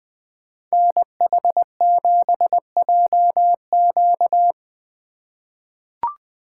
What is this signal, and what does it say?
Morse code, heard as a keyed tone, about 20 wpm, 700 Hz. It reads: NH7JQ
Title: USA Call Signs - 5 Characters - 2x Character Spacing - 20wpm